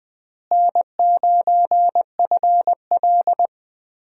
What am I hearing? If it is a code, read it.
Morse code, heard as a keyed tone, about 20 wpm, 700 Hz. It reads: N9FL
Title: USA Call Signs - 4 Characters - Repeated 3x - 20wpm